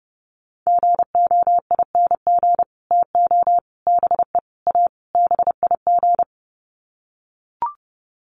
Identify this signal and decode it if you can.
Morse code, heard as a keyed tone, about 30 wpm, 700 Hz. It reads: GOING TO BE A BIG
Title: Sets of 5 Words Encore2 30wpm